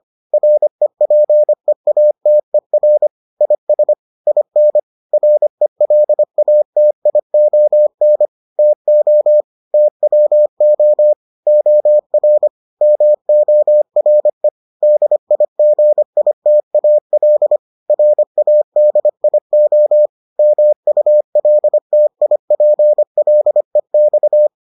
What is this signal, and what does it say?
Morse code, heard as a keyed tone, about 25 wpm, 600 Hz. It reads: REPEATER IS IN RELATION TO TWO OR MORE DIGITAL RADIO MULTIPLEX